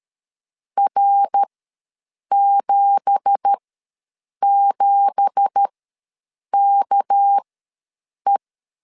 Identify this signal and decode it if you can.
Morse code, heard as a keyed tone, about 13 wpm, spaced as for 7 wpm, 800 Hz. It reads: R77KE
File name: LEZIONI / 9 2867 ws 12wpm 3431